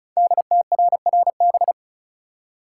Morse code, heard as a keyed tone, about 36 wpm, 700 Hz. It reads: DTRRB
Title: Code Groups - 5 Letters - 35wpm